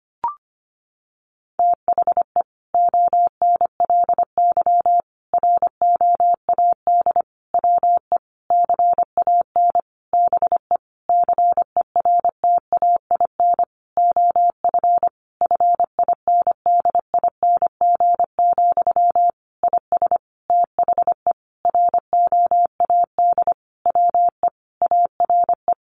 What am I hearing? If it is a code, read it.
Morse code, heard as a keyed tone, about 25 wpm, 700 Hz. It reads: THE ONLY ROAD WE CAN BE CERTAIN OF FINDING, IS THE ROAD WE ARE
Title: Sentences from Top 1100 Words 25wpm